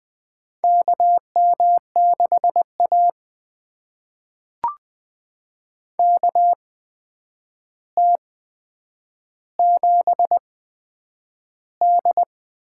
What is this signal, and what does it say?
Morse code, heard as a keyed tone, about 20 wpm, spaced as for 9 wpm, 700 Hz. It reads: KM6A KT7D
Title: USA Call Signs - 4 Characters - 8x Character Spacing - 20wpm